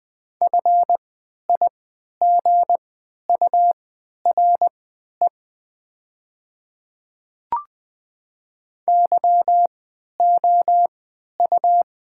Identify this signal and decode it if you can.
Morse code, heard as a keyed tone, about 20 wpm, spaced as for 11 wpm, 700 Hz. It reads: FIGURE YOU
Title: 3x Character Spacing - Top 500 Words - 20wpm